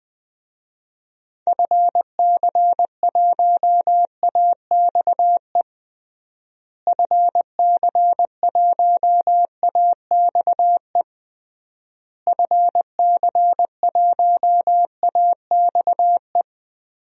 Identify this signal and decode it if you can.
Morse code, heard as a keyed tone, about 20 wpm, 700 Hz. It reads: FC1AXE FC1AXE FC1AXE
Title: DXCC-100 Call Signs - 6 Characters - Repeated 3x - 20wpm